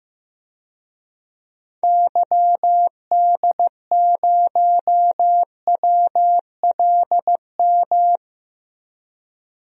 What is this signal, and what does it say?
Morse code, heard as a keyed tone, about 15 wpm, 700 Hz. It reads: YD0WLM
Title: DXCC-100 Call Signs - 6 Characters - 15wpm